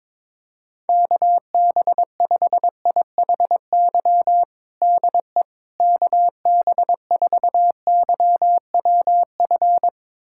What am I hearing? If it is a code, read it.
Morse code, heard as a keyed tone, about 22 wpm, 700 Hz. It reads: KB5IHY DE KB4YWF
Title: USA Callsign DE Callsign - 6 Characters - 22wpm